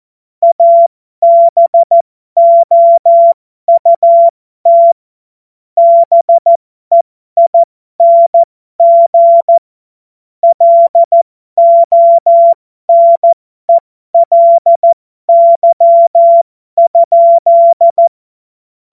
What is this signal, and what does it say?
Morse code, heard as a keyed tone, about 14 wpm, spaced as for 12 wpm, 680 Hz. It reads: ABOUT BEING LONELY?
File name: Morsekurs / Witze